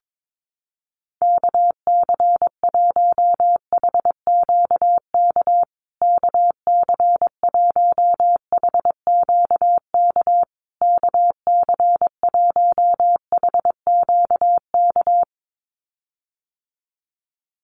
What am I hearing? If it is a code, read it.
Morse code, heard as a keyed tone, about 22 wpm, 700 Hz. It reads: KC1HQK KC1HQK KC1HQK
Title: USA Call Signs Repeated 3x 22wpm